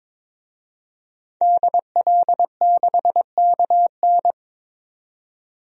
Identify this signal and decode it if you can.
Morse code, heard as a keyed tone, about 22 wpm, 700 Hz. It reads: DL6KN
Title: DXCC-100 Call Signs - 5 Characters - 22wpm